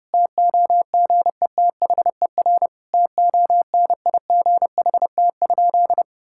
Morse code, heard as a keyed tone, about 30 wpm, 700 Hz. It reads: TOGETHER TONIGHT?
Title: Sentences from Top 1600 Words 30wpm